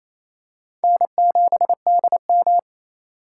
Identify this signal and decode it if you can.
Morse code, heard as a keyed tone, about 28 wpm, 700 Hz. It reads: N7DM
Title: USA Call Signs - 4 Characters - 2x Character Spacing - 28wpm